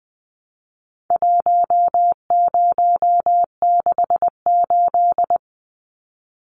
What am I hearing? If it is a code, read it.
Morse code, heard as a keyed tone, about 20 wpm, 700 Hz. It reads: 1068
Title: Numbers - 4-Digit 20wpm